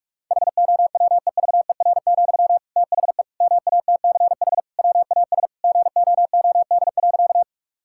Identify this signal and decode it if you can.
Morse code, heard as a keyed tone, about 46 wpm, 700 Hz. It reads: HOWEVER, THE MATCH WAS GOOD.
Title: Sentences from Top 1600 Words 45wpm